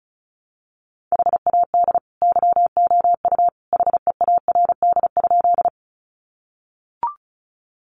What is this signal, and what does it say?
Morse code, heard as a keyed tone, about 34 wpm, 700 Hz. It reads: HAD YOU HEARD?
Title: Sentences from Top 400 Words 35wpm